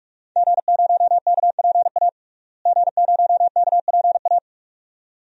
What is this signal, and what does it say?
Morse code, heard as a keyed tone, about 46 wpm, 700 Hz. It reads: G0KPA G0KPA
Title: DXCC-100 Call Signs - 5 Characters - Repeated 2x - 45wpm